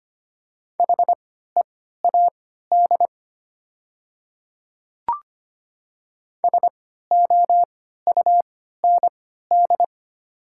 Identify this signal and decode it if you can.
Morse code, heard as a keyed tone, about 25 wpm, spaced as for 14 wpm, 700 Hz. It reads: HEAD SOUND
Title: 3x Character Spacing - Top 300 Words - 25wpm